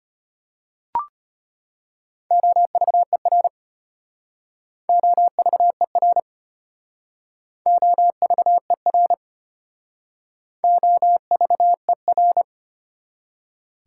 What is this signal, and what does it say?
Morse code, heard as a keyed tone, about 34 wpm, 700 Hz. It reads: OVER OVER OVER OVER